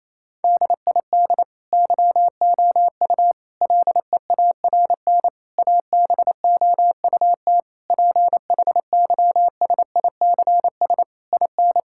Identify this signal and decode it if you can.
Morse code, heard as a keyed tone, about 28 wpm, 700 Hz. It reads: DID YOU LEARN ABOUT PHYSICS IN
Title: Sentences from Top 1000 Words - Review Part 5 - 28wpm